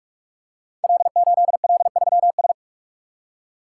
Morse code, heard as a keyed tone, about 44 wpm, 700 Hz. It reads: L8L3S
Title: Code Groups - 5 Letter:Numbers - 45wpm